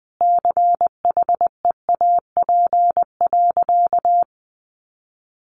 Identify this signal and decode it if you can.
Morse code, heard as a keyed tone, about 20 wpm, 700 Hz. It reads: CHEAP.
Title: Sentences from Top 1200 Words 20wpm